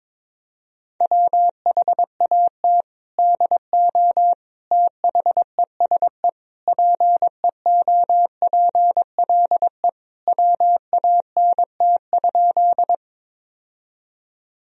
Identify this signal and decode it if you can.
Morse code, heard as a keyed tone, about 22 wpm, 700 Hz. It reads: WHAT DO THESE PEOPLE WANT?
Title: Sentences from Top 1000 Words - Review Part 1 - 22wpm